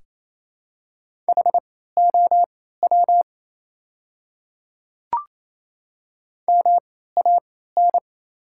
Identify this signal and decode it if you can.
Morse code, heard as a keyed tone, about 28 wpm, spaced as for 16 wpm, 700 Hz. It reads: HOW MAN
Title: 3x Character Spacing - Top 300 Words - 28wpm